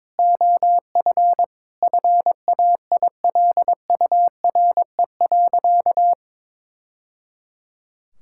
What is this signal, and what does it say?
Morse code, heard as a keyed tone, about 22 wpm, 700 Hz. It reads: OF FAILURE.
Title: Sentences from Top 1000 Words - Review Part 6 - 22wpm